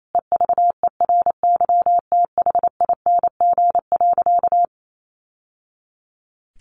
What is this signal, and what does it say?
Morse code, heard as a keyed tone, about 28 wpm, 700 Hz. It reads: EVERYTHING.
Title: Sentences from Top 1200 Words 28wpm